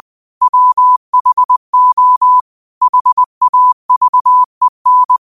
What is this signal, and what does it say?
Morse code, heard as a keyed tone, about 20 wpm, 1000 Hz. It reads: WHO HAVEN